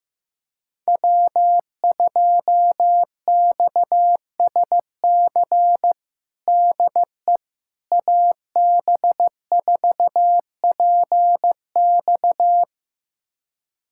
Morse code, heard as a keyed tone, about 16 wpm, 700 Hz. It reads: W2XSC DE AB4PX
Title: USA Callsign DE Callsign - 5 Characters - 15wpm